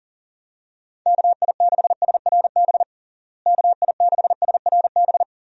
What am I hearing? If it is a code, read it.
Morse code, heard as a keyed tone, about 40 wpm, 700 Hz. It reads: KI6SRB KI6SRB
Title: USA Call Signs - 6 Characters - Repeated 3x - 40wpm